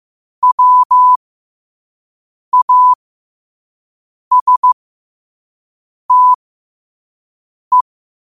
Morse code, heard as a keyed tone, about 15 wpm, spaced as for 5 wpm, 1000 Hz. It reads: WASTE
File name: qotdfiles / 5wpm